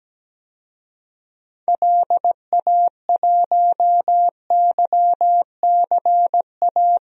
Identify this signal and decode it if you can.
Morse code, heard as a keyed tone, about 17 wpm, 700 Hz. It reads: LA1YCA
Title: DXCC-100 Call Signs - 6 Characters - 17wpm